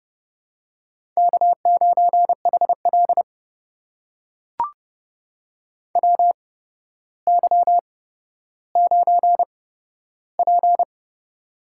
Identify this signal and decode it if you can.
Morse code, heard as a keyed tone, about 28 wpm, spaced as for 13 wpm, 700 Hz. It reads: K9HL WY9P
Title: USA Call Signs - 4 Characters - 8x Character Spacing - 30wpm